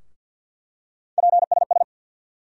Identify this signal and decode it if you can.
Morse code, heard as a keyed tone, about 49 wpm, 700 Hz. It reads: PSS
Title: USA Call Sign Postfixes - 3 Letters 50wpm